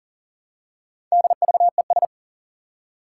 Morse code, heard as a keyed tone, about 40 wpm, 700 Hz. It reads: DVES
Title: Four Challenging Letter-Number Combinations 40wpm